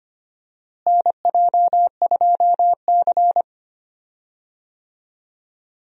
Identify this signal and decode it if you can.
Morse code, heard as a keyed tone, about 25 wpm, 700 Hz. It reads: NJ2C
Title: USA Call Signs - 4 Characters - Repeated 3x - 25wpm